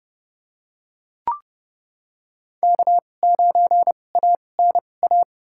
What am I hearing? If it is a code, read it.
Morse code, heard as a keyed tone, about 30 wpm, spaced as for 22 wpm, 700 Hz. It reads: K9ANA
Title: USA Call Signs - 5 Characters - 2x Character Spacing - 30wpm